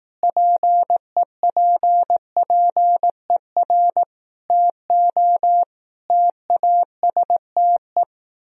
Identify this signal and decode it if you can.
Morse code, heard as a keyed tone, about 18 wpm, 700 Hz. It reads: PEPPER TO TASTE
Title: Speed-Racing - Sets of 5 Words 15wpm